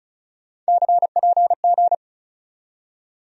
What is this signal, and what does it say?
Morse code, heard as a keyed tone, about 34 wpm, 700 Hz. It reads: CPG
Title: USA Call Sign Postfixes - 3 Letters 35wpm